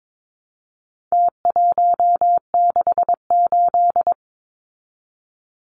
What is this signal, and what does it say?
Morse code, heard as a keyed tone, about 22 wpm, 700 Hz. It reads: T168
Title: Numbers - Contest Style - 4-Digit 22wpm